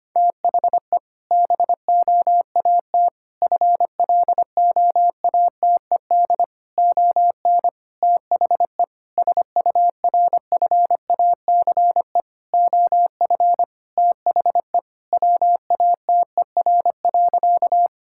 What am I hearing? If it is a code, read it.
Morse code, heard as a keyed tone, about 25 wpm, 700 Hz. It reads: THE BOAT FLOATED ON THE SURFACE OF THE WATER.
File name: practice / 25wpm / Sentences with 4-Word Phrases - 25wpm